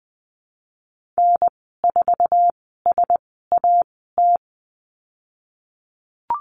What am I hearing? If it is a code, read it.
Morse code, heard as a keyed tone, about 20 wpm, spaced as for 15 wpm, 700 Hz. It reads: N4SAT
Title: 069 Lesson 12g - Letter d - Call Signs 20wpm